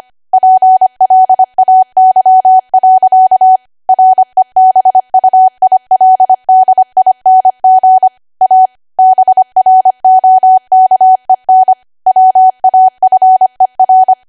Morse code, heard as a keyed tone, about 25 wpm, 750 Hz. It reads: PLAY. REBUILDING A BROKEN WAFER
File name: Archive / 25 WPM / 25WPM